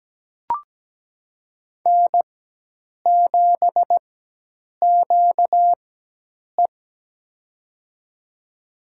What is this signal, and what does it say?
Morse code, heard as a keyed tone, about 17 wpm, spaced as for 8 wpm, 700 Hz. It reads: N7QE
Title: USA Call Signs - 4 Characters - 4x Character Spacing - 17wpm